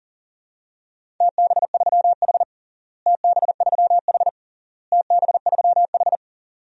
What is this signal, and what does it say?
Morse code, heard as a keyed tone, about 40 wpm, 700 Hz. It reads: TB3H TB3H TB3H
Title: DXCC-100 Call Signs - 4 Characters - Repeated 3x - 40wpm